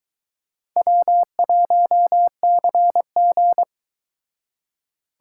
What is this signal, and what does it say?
Morse code, heard as a keyed tone, about 22 wpm, 700 Hz. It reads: W1CG